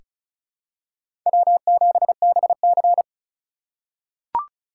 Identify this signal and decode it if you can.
Morse code, heard as a keyed tone, about 36 wpm, 700 Hz. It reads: W7BC
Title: USA Call Signs - 4 Characters - 2x Character Spacing - 35wpm